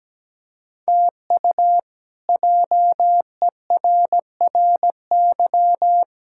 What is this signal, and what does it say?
Morse code, heard as a keyed tone, about 17 wpm, 700 Hz. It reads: TU JERRY
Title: K1USN SST QSO Practice - 17wpm